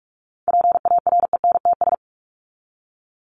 Morse code, heard as a keyed tone, about 44 wpm, 700 Hz. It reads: PARENTS
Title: ES:S Suffix - Top 100 Words with Base Words 45wpm